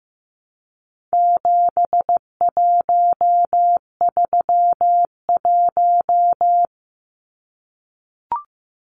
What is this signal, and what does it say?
Morse code, heard as a keyed tone, about 15 wpm, 700 Hz. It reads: 7131
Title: Numbers - 4-Digit 15wpm